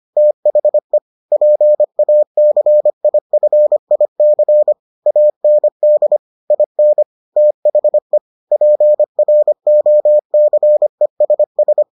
Morse code, heard as a keyed tone, about 25 wpm, 600 Hz. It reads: THE PACIFIC AND IN THE PROCESS